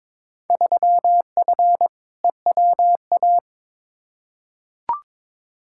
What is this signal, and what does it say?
Morse code, heard as a keyed tone, about 22 wpm, 700 Hz. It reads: 3F EWA
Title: Field Day Exchanges Encore - 22wpm